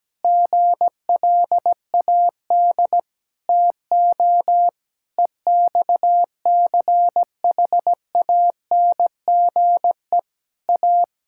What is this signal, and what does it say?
Morse code, heard as a keyed tone, about 17 wpm, 700 Hz. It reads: GLAD TO EXCHANGE A
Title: Sentences from Top 1000 Words - Review Part 6 - 17wpm